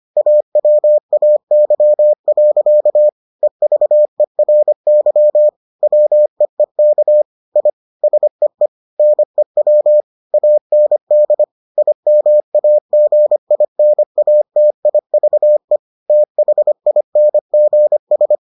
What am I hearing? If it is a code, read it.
Morse code, heard as a keyed tone, about 25 wpm, 600 Hz. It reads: AWAY. EVERY WEEK I SEE NEW AND IMAGINATIVE THINGS